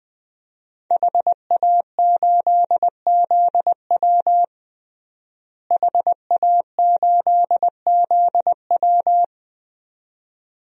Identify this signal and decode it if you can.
Morse code, heard as a keyed tone, about 20 wpm, 700 Hz. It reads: HA8ZW HA8ZW
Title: DXCC-100 Call Signs - 5 Characters - Repeated 2x - 20wpm